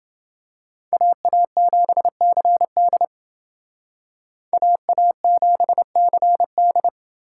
Morse code, heard as a keyed tone, about 28 wpm, 700 Hz. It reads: AA7CD AA7CD